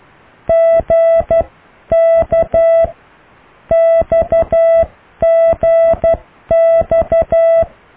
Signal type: Morse code